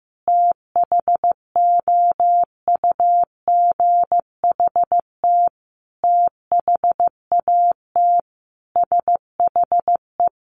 Morse code, heard as a keyed tone, about 15 wpm, 700 Hz. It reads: THOUGHT THAT SHE
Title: Sentences from Top 1200 Words 15wpm